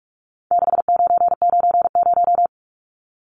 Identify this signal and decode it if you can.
Morse code, heard as a keyed tone, about 44 wpm, 700 Hz. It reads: 6990
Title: Numbers - 4-Digit 45wpm